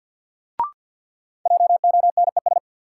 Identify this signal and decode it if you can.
Morse code, heard as a keyed tone, about 46 wpm, 700 Hz. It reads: JONES
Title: Sets of 5 Words - New Word Drill - 50wpm